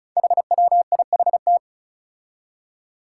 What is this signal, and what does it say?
Morse code, heard as a keyed tone, about 36 wpm, 700 Hz. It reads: HWIHT
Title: Five Challenging Letter-Number Combinations 35wpm